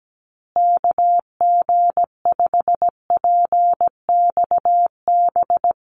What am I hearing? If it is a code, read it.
Morse code, heard as a keyed tone, about 17 wpm, 700 Hz. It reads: KG5PXB
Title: USA Call Signs Repeated 2x 17wpm